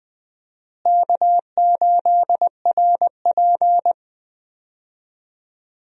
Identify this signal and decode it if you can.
Morse code, heard as a keyed tone, about 21 wpm, 700 Hz. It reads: K8RP
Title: USA Call Signs - 4 Characters - Repeated 3x - 20wpm